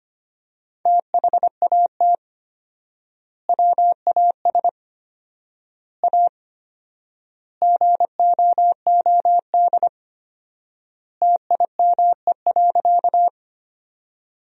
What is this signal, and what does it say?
Morse code, heard as a keyed tone, about 25 wpm, 700 Hz. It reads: THAT WAS A GOOD TIME.
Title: 4x Word-Spacing - Sentences from Top 200 Words - 25wpm